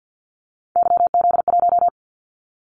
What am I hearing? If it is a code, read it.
Morse code, heard as a keyed tone, about 50 wpm, 700 Hz. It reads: YZ1
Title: Callsign Prefixes for Top 25 DXCC - 50wpm